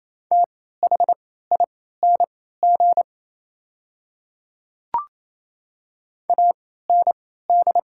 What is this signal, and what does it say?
Morse code, heard as a keyed tone, about 28 wpm, spaced as for 16 wpm, 700 Hz. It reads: THING AND
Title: 3x Character Spacing - Top 300 Words - 28wpm